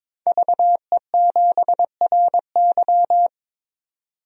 Morse code, heard as a keyed tone, about 21 wpm, 700 Hz. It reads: VE7RY
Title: DXCC-100 Call Signs - 5 Characters - 22wpm